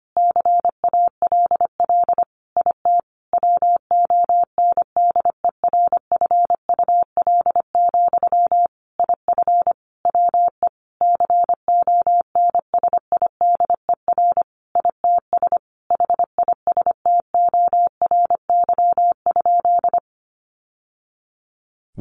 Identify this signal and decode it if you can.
Morse code, heard as a keyed tone, about 25 wpm, 700 Hz. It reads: CALL IT WONDERFUL, IF WE CONSIDER ITS HISTORY?
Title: Sentences from Top 1100 Words 25wpm